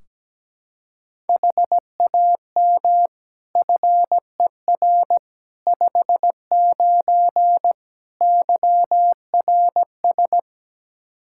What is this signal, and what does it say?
Morse code, heard as a keyed tone, about 17 wpm, 700 Hz. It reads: HAM FER 59 YRS
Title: Practice QSO - Exchange 3 - More Info - 17wpm